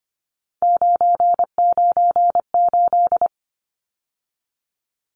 Morse code, heard as a keyed tone, about 25 wpm, 700 Hz. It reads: 998
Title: Numbers - 3-Digit 25wpm